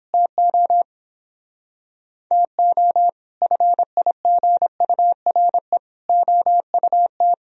Morse code, heard as a keyed tone, about 29 wpm, 700 Hz. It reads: TO TO FIGURE OUT